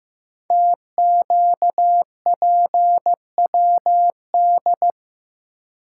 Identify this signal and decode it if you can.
Morse code, heard as a keyed tone, about 15 wpm, 700 Hz. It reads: TQPWD
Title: Code Groups - 5 Letters - 15wpm